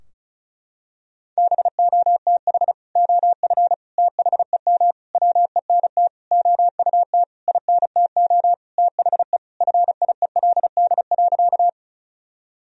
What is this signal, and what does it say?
Morse code, heard as a keyed tone, about 34 wpm, 700 Hz. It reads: BOTH OF THEM WENT OUT INTO THE FIELD.
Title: Sentences from Top 1000 Words - Review Part 2 - 35wpm